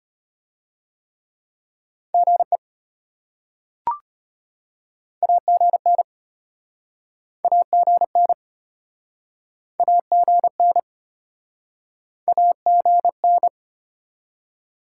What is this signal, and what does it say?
Morse code, heard as a keyed tone, about 34 wpm, 700 Hz. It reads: GE AGN AGN AGN AGN